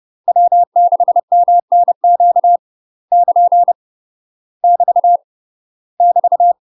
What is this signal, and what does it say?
Morse code, heard as a keyed tone, about 30 wpm, 700 Hz. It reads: W6MNQ <KN> <BT> <BT>
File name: UK MASTER QSOs TRAINING / 30WPM / UK MASTER QSOs 0040